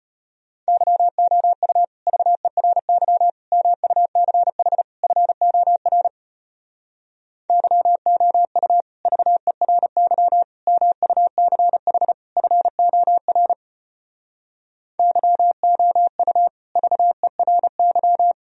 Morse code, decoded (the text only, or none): YOU VERY MUCH FOR YOU VERY MUCH FOR YOU VERY